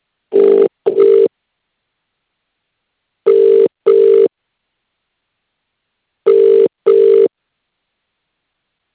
Ringback tone (UK), 400 + 450 Hz.